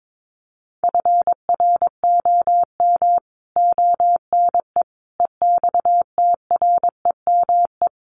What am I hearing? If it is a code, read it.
Morse code, heard as a keyed tone, about 22 wpm, 700 Hz. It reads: FROM ONE EXTREME
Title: Sentences from Top 1300 Words 22wpm